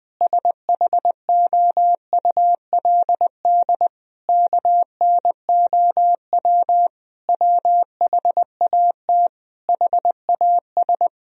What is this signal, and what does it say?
Morse code, heard as a keyed tone, about 20 wpm, 700 Hz. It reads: SHOULD KNOW WHAT HAS
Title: Sentences from Top 1000 Words - Review Part 1 - 20wpm